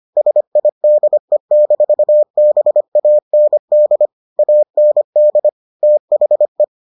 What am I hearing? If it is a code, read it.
Morse code, heard as a keyed tone, about 25 wpm, 600 Hz. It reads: SIDE-BAND AND THE